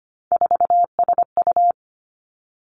Morse code, heard as a keyed tone, about 25 wpm, 700 Hz. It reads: 4SU
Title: Three Challenging Letter-Number Combinations 25wpm